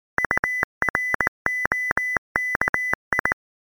Morse code (Russian alphabet), heard as a keyed tone, about 19 wpm, 2000 Hz. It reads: УРОКИ